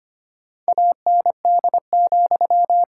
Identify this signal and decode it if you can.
Morse code, heard as a keyed tone, about 25 wpm, 700 Hz. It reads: AND,
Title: Sentences from Top 1000 Words - Review Part 5 - 25wpm